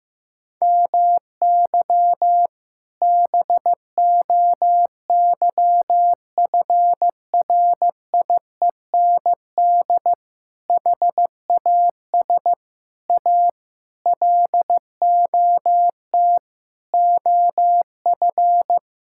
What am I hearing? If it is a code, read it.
Morse code, heard as a keyed tone, about 16 wpm, 700 Hz. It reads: MY BOYFRIEND HAS A LOT OF
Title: Sentences from Top 1000 Words - Review Part 5 - 15wpm